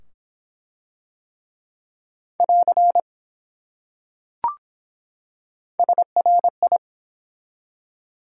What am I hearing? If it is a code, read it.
Morse code, heard as a keyed tone, about 23 wpm, 700 Hz. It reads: <AR> SRI